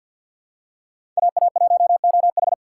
Morse code, heard as a keyed tone, about 48 wpm, 700 Hz. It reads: AA1OH
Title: USA Call Signs - 5 Characters - 3x Character Spacing - 50wpm